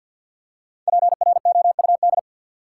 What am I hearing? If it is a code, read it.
Morse code, heard as a keyed tone, about 47 wpm, 700 Hz. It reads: PROUD
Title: Five Letter Words - 50wpm